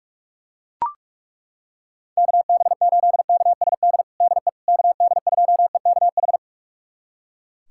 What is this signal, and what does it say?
Morse code, heard as a keyed tone, about 43 wpm, 700 Hz. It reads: KB8KSD DE KD2EKH
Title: USA Callsign DE Callsign 45wpm